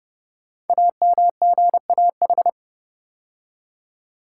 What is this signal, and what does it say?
Morse code, heard as a keyed tone, about 30 wpm, 700 Hz. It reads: AMGAH
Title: Code Groups - 5 Letters - 30wpm